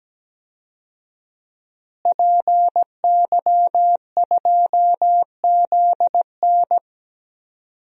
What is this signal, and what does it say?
Morse code, heard as a keyed tone, about 17 wpm, 700 Hz. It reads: PY2ZN
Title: DXCC-100 Call Signs - 5 Characters - Repeated 3x - 17wpm